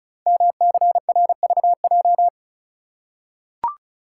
Morse code, heard as a keyed tone, about 36 wpm, 690 Hz. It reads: MCRVJ
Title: Code Groups - 5 Letters - 35wpm